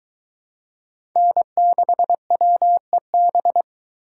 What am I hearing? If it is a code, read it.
Morse code, heard as a keyed tone, about 22 wpm, 700 Hz. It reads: N6WEB